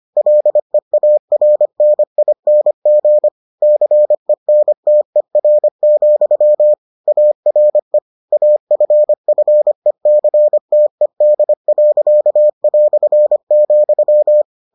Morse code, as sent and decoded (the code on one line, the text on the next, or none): .-.. .- .-. -. .. -. --. / -.-. -. - . .-. --..-- / .- .-. / .- ..-. ..-. -.-. - . -.. .-.-.- .-..-. --..--
LEARNING CENTER, ARE AFFECTED.",